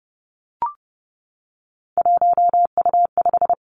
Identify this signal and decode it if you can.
Morse code, heard as a keyed tone, about 30 wpm, 700 Hz. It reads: 1U5
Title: Three Challenging Letter-Number Combinations 30wpm